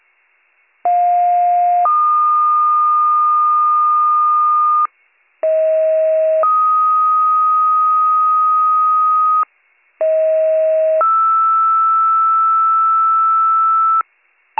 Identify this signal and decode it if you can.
Two-tone page, heard as a one-second tone, then a three-second tone, about 710 then 1220 Hz, about 640 then 1250 Hz, about 640 then 1320 Hz. It is heard from 0.8 to 14.0 s.